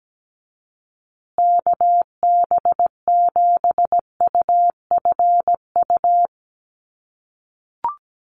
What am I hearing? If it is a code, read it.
Morse code, heard as a keyed tone, about 17 wpm, 700 Hz. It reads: KB7UFU
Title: USA Call Signs Repeated 3x 17wpm